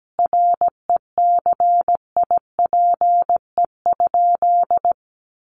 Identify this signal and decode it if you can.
Morse code, heard as a keyed tone, about 17 wpm, 700 Hz. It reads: RECIPE?
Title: Sentences from Top 1200 Words 17wpm